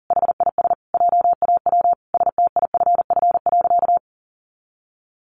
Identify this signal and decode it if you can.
Morse code, heard as a keyed tone, about 40 wpm, 700 Hz. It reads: HIS JAW STIFF.
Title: A Sound Of Thunder 40wpm